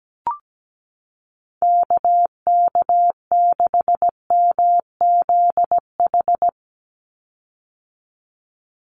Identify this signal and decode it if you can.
Morse code, heard as a keyed tone, about 17 wpm, 700 Hz. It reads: KK6MZH
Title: USA Call Signs 17wpm